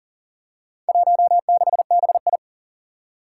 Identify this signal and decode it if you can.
Morse code, heard as a keyed tone, about 39 wpm, 700 Hz. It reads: 16BI
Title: Four Challenging Letter-Number Combinations 40wpm